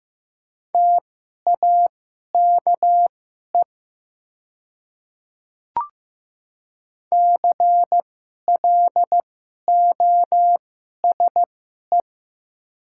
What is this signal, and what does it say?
Morse code, heard as a keyed tone, about 15 wpm, spaced as for 11 wpm, 700 Hz. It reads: TAKE CLOSE
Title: 2x Character Spacing - Top 500 Words - 15wpm